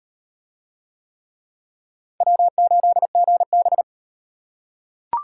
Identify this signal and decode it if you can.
Morse code, heard as a keyed tone, about 37 wpm, 700 Hz. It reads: W8GB